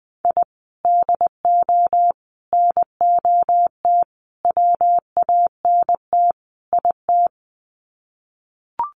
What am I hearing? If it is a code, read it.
Morse code, heard as a keyed tone, about 20 wpm, 700 Hz. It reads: I DO NOT WANT IT
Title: Sets of 5 Words Encore2 20wpm